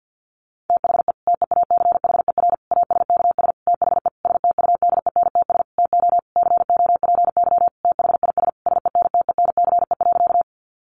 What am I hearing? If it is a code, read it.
Morse code, heard as a keyed tone, about 50 wpm, 700 Hz. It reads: THE TEACHER ASKS THE STUDENTS TO COPY THIS SENTENCE.
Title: Additional Easy Phrases 50wpm